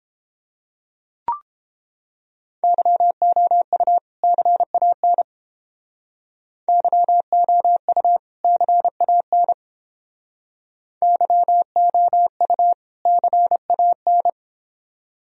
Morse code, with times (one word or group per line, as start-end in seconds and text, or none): 2.6-4.0 YOU
4.2-5.2 CAN
6.7-8.2 YOU
8.4-9.5 CAN
11.0-12.7 YOU
13.1-14.3 CAN